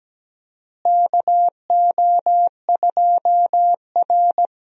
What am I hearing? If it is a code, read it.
Morse code, heard as a keyed tone, about 17 wpm, 700 Hz. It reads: KO2R
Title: USA Call Signs - 4 Characters - Repeated 3x - 17wpm